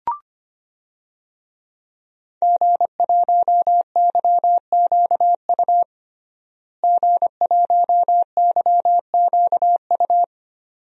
Morse code, heard as a keyed tone, about 25 wpm, 700 Hz. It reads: G1YQU G1YQU
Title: DXCC-100 Call Signs - 5 Characters - Repeated 2x - 25wpm